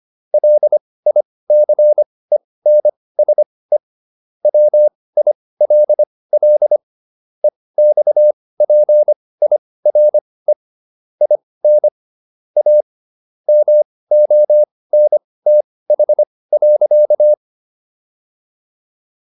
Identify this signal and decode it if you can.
Morse code, heard as a keyed tone, about 25 wpm, spaced as for 18 wpm, 600 Hz. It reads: LICENSE WILL EXPIRE IN A MONTH.